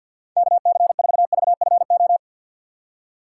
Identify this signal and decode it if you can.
Morse code, heard as a keyed tone, about 53 wpm, 700 Hz. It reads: KC4VFO
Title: USA Call Signs - 6 Characters - 6x Character Spacing - 50wpm